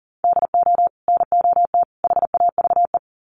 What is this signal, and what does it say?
Morse code, heard as a keyed tone, about 40 wpm, 700 Hz. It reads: DO NOT HAVE
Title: Sets of 3 Words 40wpm